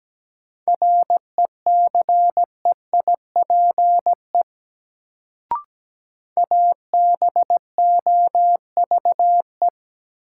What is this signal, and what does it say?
Morse code, heard as a keyed tone, about 17 wpm, 700 Hz. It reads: RECEIPE ABOVE
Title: Sentences from Top 1200 Words - New Word Drill Alt - 17wpm